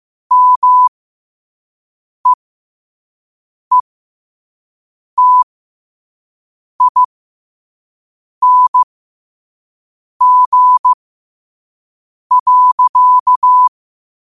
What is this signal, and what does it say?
Morse code, heard as a keyed tone, about 15 wpm, spaced as for 5 wpm, 1000 Hz. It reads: MEETING.
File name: qotdfiles / 5wpm